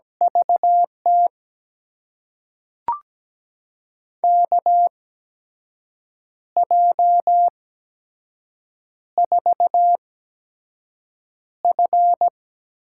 Morse code, heard as a keyed tone, about 17 wpm, spaced as for 6 wpm, 700 Hz. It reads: VT KJ4F
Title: USA Call Signs - 6 Characters - 8x Character Spacing - 17wpm